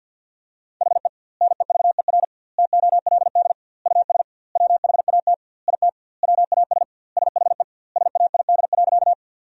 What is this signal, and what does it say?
Morse code, heard as a keyed tone, about 50 wpm, 700 Hz. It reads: HE NEVER TOLD US WHAT IT WAS SHE SAID.
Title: 2x Word-Spacing - Sentences from Top 400 Words - 50wpm